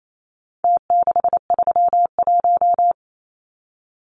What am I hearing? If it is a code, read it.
Morse code, heard as a keyed tone, about 28 wpm, 700 Hz. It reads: T631
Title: Numbers - Contest Style - 4-Digit 28wpm